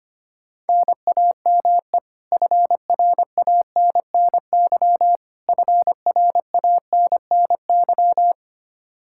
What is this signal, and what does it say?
Morse code, heard as a keyed tone, about 25 wpm, 700 Hz. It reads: NAME FRANNY FRANNY
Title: Practice QSO - Exchange 1 - Beginning - 25wpm